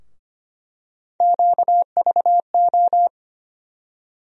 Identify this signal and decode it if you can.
Morse code, heard as a keyed tone, about 25 wpm, 700 Hz. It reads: QVO
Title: USA Call Sign Postfixes - 3 Letters 25wpm